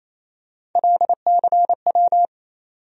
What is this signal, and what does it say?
Morse code, heard as a keyed tone, about 27 wpm, 700 Hz. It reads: LCW
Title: USA Call Sign Postfixes - 3 Letters 28wpm